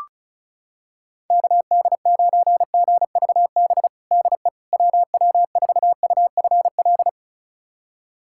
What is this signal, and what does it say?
Morse code, heard as a keyed tone, about 36 wpm, 700 Hz. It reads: KD9GVB DE WW4UFL
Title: USA Callsign DE Callsign - 6 Characters - 35wpm